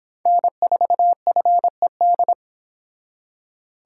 Morse code, heard as a keyed tone, about 26 wpm, 700 Hz. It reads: N4FED